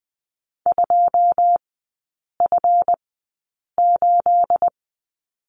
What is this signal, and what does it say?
Morse code, heard as a keyed tone, about 20 wpm, spaced as for 8 wpm, 700 Hz. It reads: 2F8
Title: 212 Lesson 34e - Number 8 - Three Characters 20wpm